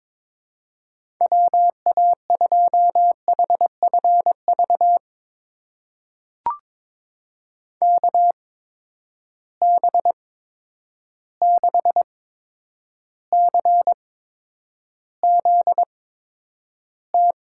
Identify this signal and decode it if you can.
Morse code, heard as a keyed tone, about 22 wpm, spaced as for 9 wpm, 700 Hz. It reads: WA2HFV KB6CZT
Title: USA Call Signs - 6 Characters - 8x Character Spacing - 22wpm